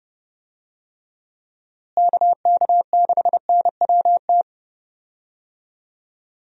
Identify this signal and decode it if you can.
Morse code, heard as a keyed tone, about 30 wpm, 700 Hz. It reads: KK6NWT
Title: USA Call Signs - 6 Characters - Repeated 2x - 30wpm